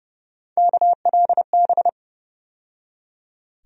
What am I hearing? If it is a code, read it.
Morse code, heard as a keyed tone, about 30 wpm, 700 Hz. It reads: KLB
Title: USA Call Sign Postfixes - 3 Letters 30wpm